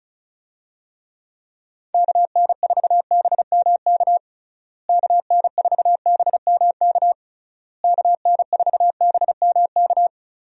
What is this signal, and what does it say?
Morse code, heard as a keyed tone, about 36 wpm, 700 Hz. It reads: KN4BMK KN4BMK KN4BMK
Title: USA Call Signs - 6 Characters - Repeated 3x - 35wpm